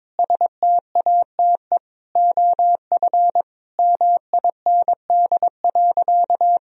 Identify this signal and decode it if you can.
Morse code, heard as a keyed tone, about 22 wpm, 700 Hz. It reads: STATE OF MIND.
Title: Sentences from Top 1000 Words - Review Part 2 - 22wpm